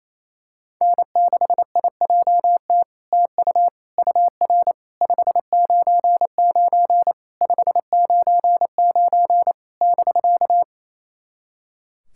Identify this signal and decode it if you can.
Morse code, heard as a keyed tone, about 28 wpm, 700 Hz. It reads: N6IJT TU UR 599 599 <BK>
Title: SOTA QSO Practice 28wpm